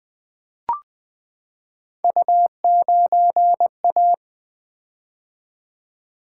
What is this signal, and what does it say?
Morse code, heard as a keyed tone, about 20 wpm, 700 Hz. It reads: U9A
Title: Three Letter-Number Combinations - 20wpm